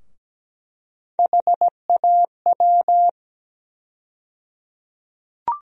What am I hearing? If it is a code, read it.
Morse code, heard as a keyed tone, about 17 wpm, 700 Hz. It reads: HAW
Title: USA Call Sign Postfixes - 3 Letters 17wpm